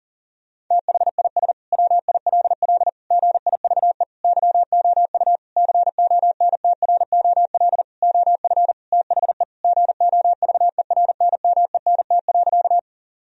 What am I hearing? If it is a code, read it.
Morse code, heard as a keyed tone, about 40 wpm, 700 Hz. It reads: THIS WILL GIVE YOU CONTROL OF THE GOVERNMENT.
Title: Sentences from Top 1000 Words - Review Part 3 - 40wpm